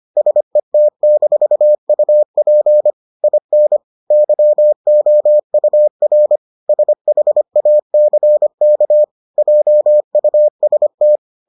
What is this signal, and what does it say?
Morse code, heard as a keyed tone, about 25 wpm, 600 Hz. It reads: SET-UP IN YOUR SHACK JUST